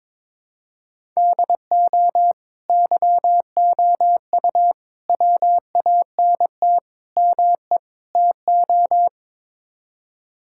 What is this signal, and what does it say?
Morse code, heard as a keyed tone, about 22 wpm, 700 Hz. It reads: DO YOU WANT ME TO
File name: practice / 22wpm / Build A Phrase - 5-Word Phrases - 22wpm